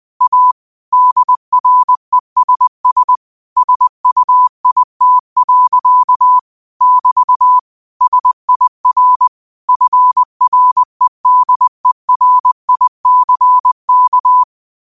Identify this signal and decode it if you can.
Morse code, heard as a keyed tone, about 20 wpm, 1000 Hz. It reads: A DRESS SUIT. <BT> SIR FREDERICK
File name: qotdfiles / 20wpm